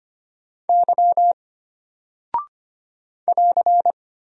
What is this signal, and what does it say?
Morse code, heard as a keyed tone, about 25 wpm, 700 Hz. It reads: Y <AR>
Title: Single Character (Letter:Number:Punctuation:Prosign) Alt 25wpm